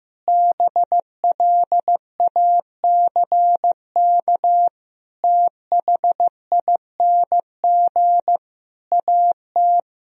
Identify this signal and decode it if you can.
Morse code, heard as a keyed tone, about 15 wpm, 700 Hz. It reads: BLACK THING AT
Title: Sentences from Top 1000 Words - Review Part 2 - 15wpm